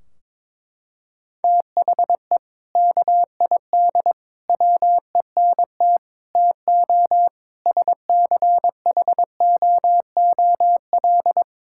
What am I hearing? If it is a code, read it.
Morse code, heard as a keyed tone, about 22 wpm, 700 Hz. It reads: THE KID WENT TO SCHOOL
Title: Sentences from Top 1600 Words 22wpm